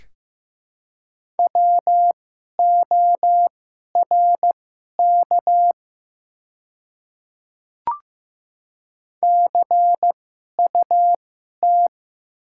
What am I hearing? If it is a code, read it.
Morse code, heard as a keyed tone, about 15 wpm, spaced as for 11 wpm, 700 Hz. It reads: WORK CUT
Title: 2x Character Spacing - Top 300 Words - 15wpm